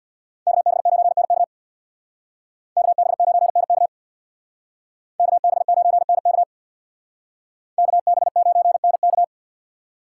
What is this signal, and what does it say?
Morse code, heard as a keyed tone, about 60 wpm, 700 Hz. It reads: KB9NK KB9NK KB9NK KB9NK
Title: Speed Racing - USA 2x2 Call Signs 50wpm